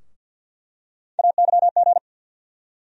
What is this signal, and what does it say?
Morse code, heard as a keyed tone, about 49 wpm, 700 Hz. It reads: AYG
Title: USA Call Sign Postfixes - 3 Letters 50wpm